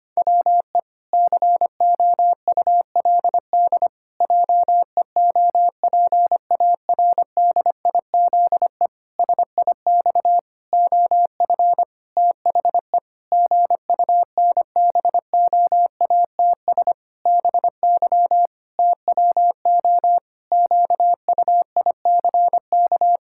Morse code, heard as a keyed tone, about 25 wpm, 700 Hz. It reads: WE COULD JEOPARDIZE SIX OF THE GUNBOATS BY TWO QUICK
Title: Pangram Challenge v2 - 25wpm